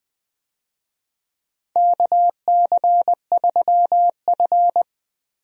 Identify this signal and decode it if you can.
Morse code, heard as a keyed tone, about 20 wpm, 700 Hz. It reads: KC3F
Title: USA Call Signs - 4 Characters - Repeated 3x - 20wpm